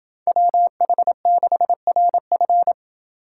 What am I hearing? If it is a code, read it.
Morse code, heard as a keyed tone, about 30 wpm, 700 Hz. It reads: WH6RF